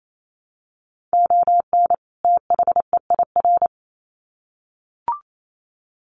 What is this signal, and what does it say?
Morse code, heard as a keyed tone, about 28 wpm, 700 Hz. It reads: ON THEIR
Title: Sets of 2 Words 28wpm